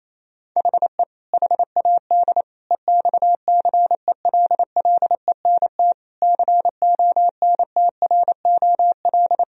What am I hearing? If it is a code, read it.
Morse code, heard as a keyed tone, about 28 wpm, 700 Hz. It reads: HE HAD EXCELLENT CONTROL
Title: Sentences from Top 1000 Words - Review Part 3 - 28wpm